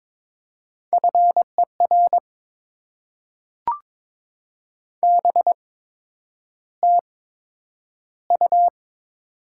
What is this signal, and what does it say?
Morse code, heard as a keyed tone, about 22 wpm, spaced as for 9 wpm, 700 Hz. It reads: FER BTU
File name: Top 90 QSO Elements - 8x Character Spacing - 22wpm